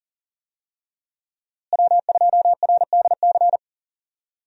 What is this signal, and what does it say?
Morse code, heard as a keyed tone, about 40 wpm, 700 Hz. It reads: W2RDC
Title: USA Call Signs - 5 Characters - Repeated 3x - 40wpm